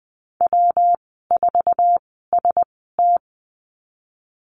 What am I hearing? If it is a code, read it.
Morse code, heard as a keyed tone, about 20 wpm, spaced as for 15 wpm, 700 Hz. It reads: W4ST
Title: 147 Lesson 24g - Letter y - Call Signs 20wpm